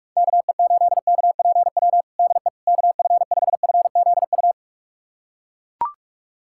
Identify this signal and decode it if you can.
Morse code, heard as a keyed tone, about 41 wpm, 700 Hz. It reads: KE8KPW DE KF5FZU
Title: USA Callsign DE Callsign 45wpm